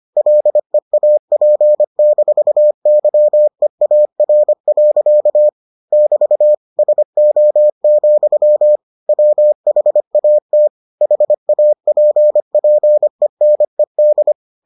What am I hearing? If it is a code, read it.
Morse code, heard as a keyed tone, about 25 wpm, 600 Hz. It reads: LEAP-YEAR. <BT> SO, WHAT HAPPENED